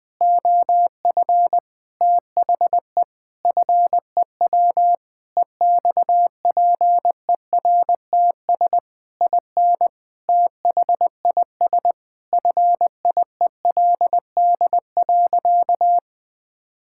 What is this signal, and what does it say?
Morse code, heard as a keyed tone, about 21 wpm, 700 Hz. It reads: OF THE FEW EXPERTS IN THIS FIELD.
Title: Sentences with 5-Word Phrases - 20wpm